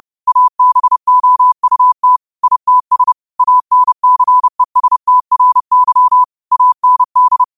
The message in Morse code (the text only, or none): ABOUT ITS ANCESTRY AND